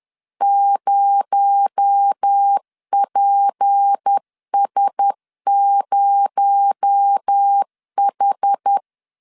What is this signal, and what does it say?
Morse code, heard as a keyed tone, about 10 wpm, 800 Hz. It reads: 0PS0H
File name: LEZIONI / 6 PKX0 / 6 PKX0 10wpm 2707